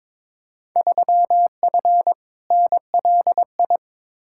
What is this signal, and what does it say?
Morse code, heard as a keyed tone, about 22 wpm, 700 Hz. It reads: 3F NLI
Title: Field Day Exchanges - 22wpm